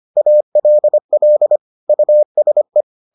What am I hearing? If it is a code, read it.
Morse code, heard as a keyed tone, about 25 wpm, 600 Hz. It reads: ALL USE